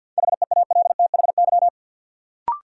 Morse code, heard as a keyed tone, about 46 wpm, 700 Hz. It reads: HEALTHY
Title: Sentences from Top 1000 Words - Review Part 5 - New Word Drill - 50wpm